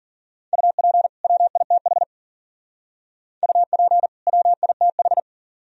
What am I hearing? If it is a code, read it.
Morse code, heard as a keyed tone, about 53 wpm, 690 Hz. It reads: UP WITH UP WITH